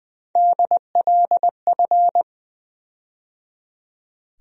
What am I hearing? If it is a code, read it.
Morse code, heard as a keyed tone, about 21 wpm, 700 Hz. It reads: DLF
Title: USA Call Sign Postfixes - 3 Letters 20wpm